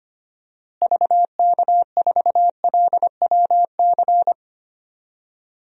Morse code, heard as a keyed tone, about 25 wpm, 700 Hz. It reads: VK4LWC
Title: DXCC-100 Call Signs - 6 Characters - Repeated 3x - 25wpm